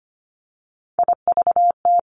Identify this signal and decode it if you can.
Morse code, heard as a keyed tone, about 25 wpm, 700 Hz. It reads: IVT
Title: Three Challenging Letter-Number Combinations 25wpm